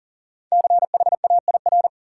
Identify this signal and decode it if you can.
Morse code, heard as a keyed tone, about 40 wpm, 700 Hz. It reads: CHAIR
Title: Five Letter Words - 40wpm